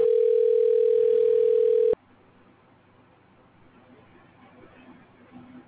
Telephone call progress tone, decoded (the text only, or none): ringback tone